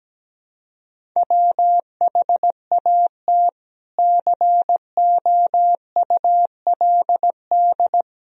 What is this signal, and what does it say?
Morse code, heard as a keyed tone, about 17 wpm, 700 Hz. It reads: WHAT COULD
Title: Sentences from Top 1000 Words - Review Part 1 - 17wpm